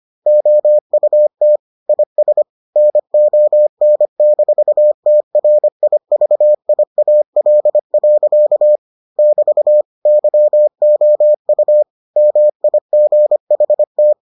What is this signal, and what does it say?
Morse code, heard as a keyed tone, about 25 wpm, 600 Hz. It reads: OUT IS NON-TRIVIAL. <BT> YOU MIGHT